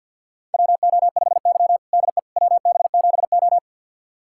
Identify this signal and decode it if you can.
Morse code, heard as a keyed tone, about 47 wpm, 700 Hz. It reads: WO5Y DE WB7O
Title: USA Callsign DE Callsign - 4 Characters - 50wpm